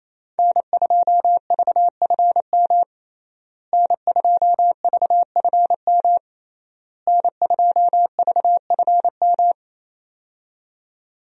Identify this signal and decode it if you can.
Morse code, heard as a keyed tone, about 28 wpm, 700 Hz. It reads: N2VFM N2VFM N2VFM
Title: USA Call Signs - 5 Characters - Repeated 3x - 28wpm